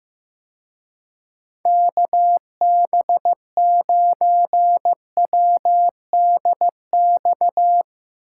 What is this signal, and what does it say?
Morse code, heard as a keyed tone, about 16 wpm, 700 Hz. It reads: KB9WDX
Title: Speed Racing - USA Call Signs 15wpm